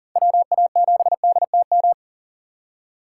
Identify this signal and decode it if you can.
Morse code, heard as a keyed tone, about 40 wpm, 700 Hz. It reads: WA7DTM
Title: USA Call Signs - 6 Characters - 3x Character Spacing - 40wpm